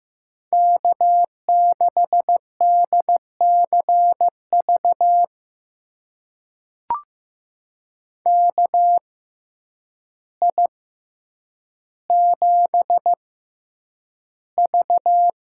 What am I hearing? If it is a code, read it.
Morse code, heard as a keyed tone, about 15 wpm, spaced as for 8 wpm, 700 Hz. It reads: K6DCV KI7V
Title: USA Call Signs - 6x Character Spacing - 15wpm